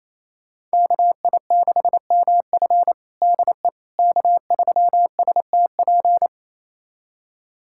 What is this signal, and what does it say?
Morse code, heard as a keyed tone, about 28 wpm, 700 Hz. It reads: KI6MF DE K3STP
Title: USA Callsign DE Callsign - 5 Characters - 28wpm